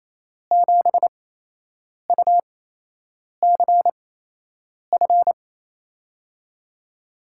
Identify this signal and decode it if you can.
Morse code, heard as a keyed tone, about 27 wpm, spaced as for 8 wpm, 700 Hz. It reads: 7UCF
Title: USA Call Signs - 6 Characters - 8x Character Spacing - 28wpm